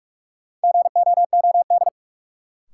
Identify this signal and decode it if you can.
Morse code, heard as a keyed tone, about 46 wpm, 700 Hz. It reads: GOOD
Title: Sentences from Top 200 Words - New Word Drill - 45wpm